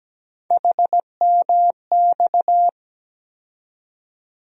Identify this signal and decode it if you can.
Morse code, heard as a keyed tone, about 17 wpm, 700 Hz. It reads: HMX
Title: USA Call Sign Postfixes - 3 Letters 17wpm